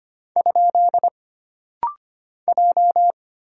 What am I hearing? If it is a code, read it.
Morse code, heard as a keyed tone, about 25 wpm, 700 Hz. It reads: ? J
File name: Single Character (Letter:Number:Punctuation:Prosign) - Rapid-Fire - 25wpm